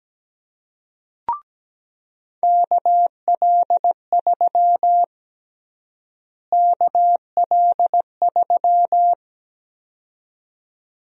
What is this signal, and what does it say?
Morse code, heard as a keyed tone, about 17 wpm, 700 Hz. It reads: KL3 KL3
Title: USA Call Sign Prefixes - Repeated 2x - 17wpm